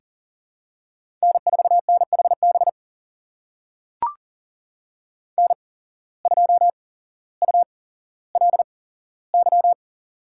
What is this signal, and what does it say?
Morse code, heard as a keyed tone, about 40 wpm, spaced as for 17 wpm, 700 Hz. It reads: N4NHB N2ULY
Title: USA Call Signs - 5 Characters - 8x Character Spacing - 40wpm